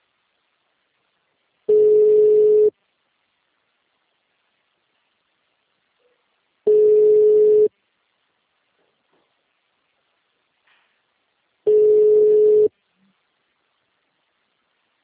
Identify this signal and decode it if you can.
Ringback tone (Europe, 425 Hz).
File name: out- -2011- - - .131